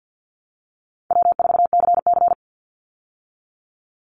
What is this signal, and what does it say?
Morse code, heard as a keyed tone, about 50 wpm, 700 Hz. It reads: W4CC
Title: Easy Call Signs Repeated 2x - 50wpm